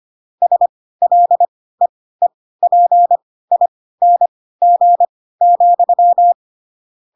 Morse code, heard as a keyed tone, about 25 wpm, spaced as for 16 wpm, 700 Hz. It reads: SLEEPING,
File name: Pinocchio / Pinocchio ch4 10 25 ik1hge